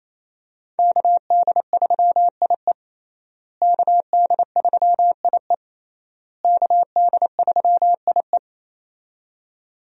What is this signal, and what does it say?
Morse code, heard as a keyed tone, about 27 wpm, 700 Hz. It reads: KD3IE KD3IE KD3IE
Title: USA Call Signs - 5 Characters - Repeated 3x - 28wpm